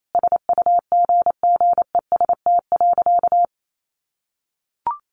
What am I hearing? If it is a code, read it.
Morse code, heard as a keyed tone, about 28 wpm, 700 Hz. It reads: SUGGEST.
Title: Sentences from Top 1500 Words 28wpm